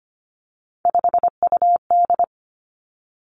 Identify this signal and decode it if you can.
Morse code, heard as a keyed tone, about 25 wpm, 700 Hz. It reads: 5UD
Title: Three Challenging Letter-Number Combinations 25wpm